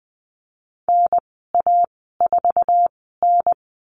Morse code, heard as a keyed tone, about 20 wpm, spaced as for 15 wpm, 700 Hz. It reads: NA4N
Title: 055 Lesson 10g - Letter r - Call Signs 20wpm